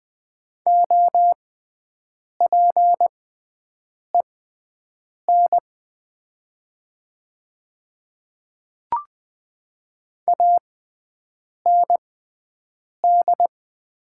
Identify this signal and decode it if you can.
Morse code, heard as a keyed tone, about 20 wpm, spaced as for 7 wpm, 700 Hz. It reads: OPEN AND
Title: 6x Character Spacing - Top 300 Words - 20wpm